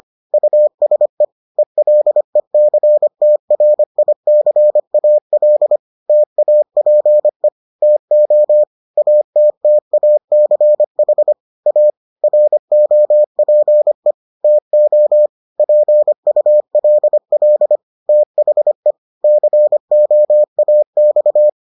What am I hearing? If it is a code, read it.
Morse code, heard as a keyed tone, about 25 wpm, 600 Hz. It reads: USE ELECTRICAL TAPE TO ATTACH A ROPE TO PULL THE COAX